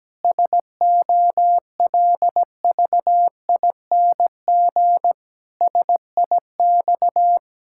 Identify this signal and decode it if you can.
Morse code, heard as a keyed tone, about 17 wpm, 700 Hz. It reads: SOLVING SIX
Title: Pangram Challenge v2 - 17wpm